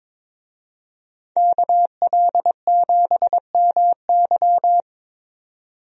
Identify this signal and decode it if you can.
Morse code, heard as a keyed tone, about 22 wpm, 700 Hz. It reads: KL7MY
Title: USA Call Signs - 5 Characters - 22wpm